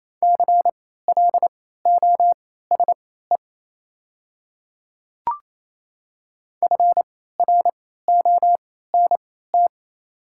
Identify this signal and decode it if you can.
Morse code, heard as a keyed tone, about 28 wpm, spaced as for 16 wpm, 700 Hz. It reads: CLOSE FRONT
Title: 3x Character Spacing - Top 500 Words - 28wpm